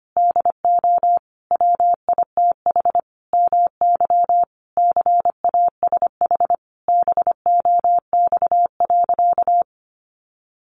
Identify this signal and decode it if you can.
Morse code, heard as a keyed tone, about 25 wpm, 700 Hz. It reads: DO WITH MY CASH BOX.
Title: Sentences from Top 700 Words 25wpm